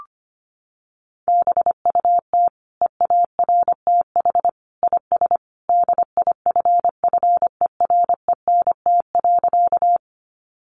Morse code, heard as a keyed tone, about 25 wpm, 700 Hz. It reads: BUT EARTH IS DIFFERENT.
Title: Sentences from Top 600 Words 25wpm